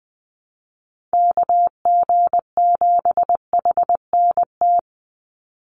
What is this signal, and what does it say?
Morse code, heard as a keyed tone, about 20 wpm, 700 Hz. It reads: KG7HNT
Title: USA Call Signs 20wpm